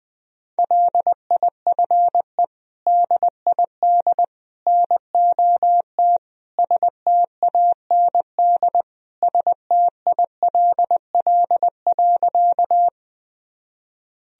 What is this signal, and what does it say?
Morse code, heard as a keyed tone, about 20 wpm, 700 Hz. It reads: LIFE DID NOT STAND STILL.
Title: Sentences from Top 1000 Words - Review Part 1 - 20wpm